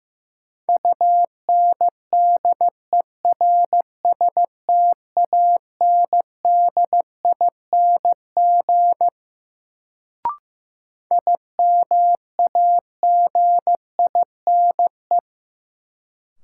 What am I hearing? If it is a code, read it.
Morse code, heard as a keyed tone, about 16 wpm, 700 Hz. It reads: UNDERSTANDING IMAGINE
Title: Sentences from Top 1400 Words - New Word Drill Alt - 15wpm